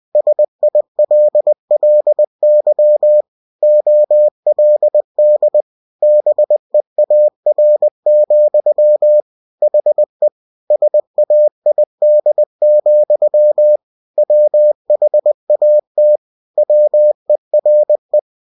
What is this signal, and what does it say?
Morse code, heard as a keyed tone, about 20 wpm, 600 Hz. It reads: SILLY OLD BEAR, HE SAID, WHAT WERE
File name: cw-books / winnie-the-pooh / winnie-20wpm-page-0025